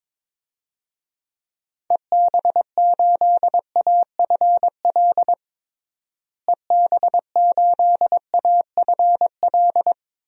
Morse code, heard as a keyed tone, about 22 wpm, 700 Hz. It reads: EB8AFL EB8AFL
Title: DXCC-100 Call Signs - 6 Characters - Repeated 2x - 22wpm